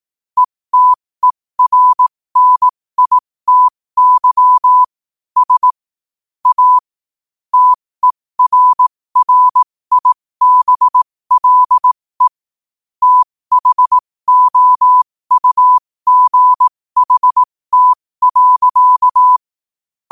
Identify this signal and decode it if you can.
Morse code, heard as a keyed tone, about 18 wpm, spaced as for 15 wpm, 1000 Hz. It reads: ETERNITY S A TERRIBLE THOUGHT.
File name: qotdfiles / 15wpm